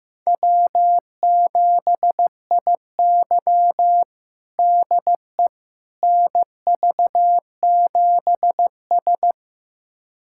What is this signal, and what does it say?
Morse code, heard as a keyed tone, about 15 wpm, 700 Hz. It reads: W7IY DE NV7S
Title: USA Callsign DE Callsign - 4 Characters - 15wpm